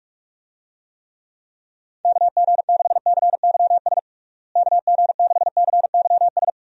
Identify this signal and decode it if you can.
Morse code, heard as a keyed tone, about 46 wpm, 700 Hz. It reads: KG6CYS KG6CYS
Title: USA Call Signs - 6 Characters - Repeated 2x - 45wpm